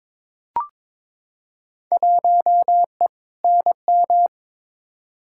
Morse code, heard as a keyed tone, about 22 wpm, 700 Hz. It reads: 1E NM
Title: Field Day Exchanges - 22wpm